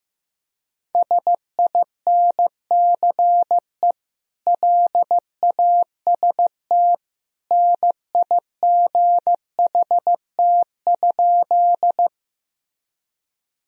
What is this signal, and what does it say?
Morse code, heard as a keyed tone, about 15 wpm, 700 Hz. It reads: SINCE LAST NIGHT?
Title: Sentences from Top 1000 Words - Review Part 1 - 15wpm